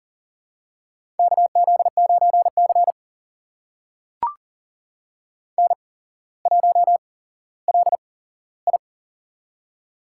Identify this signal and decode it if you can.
Morse code, heard as a keyed tone, about 40 wpm, spaced as for 17 wpm, 700 Hz. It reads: KZ9C N1LI
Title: USA Call Signs - 4 Characters - 8x Character Spacing - 40wpm